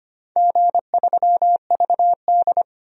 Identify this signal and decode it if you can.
Morse code, heard as a keyed tone, about 25 wpm, 700 Hz. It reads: G3VD
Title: DXCC-100 Call Signs - 4 Characters - 25wpm